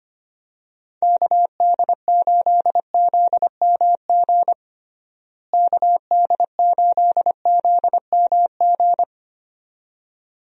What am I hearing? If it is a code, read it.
Morse code, heard as a keyed tone, about 25 wpm, 700 Hz. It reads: KD8ZMG KD8ZMG
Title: USA Call Signs - 6 Characters - Repeated 2x - 25wpm